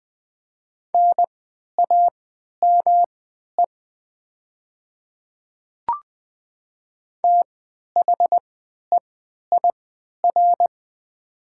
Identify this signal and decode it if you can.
Morse code, heard as a keyed tone, about 20 wpm, spaced as for 11 wpm, 700 Hz. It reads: NAME THEIR
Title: 3x Character Spacing - Top 200 Words - 20wpm